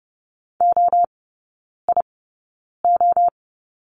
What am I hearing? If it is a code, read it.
Morse code, heard as a keyed tone, about 30 wpm, spaced as for 9 wpm, 700 Hz. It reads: OIO
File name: 081 Lesson 14e - Number 2 - Three Characters 30wpm